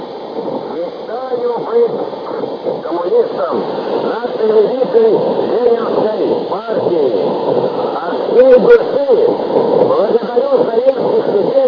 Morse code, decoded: none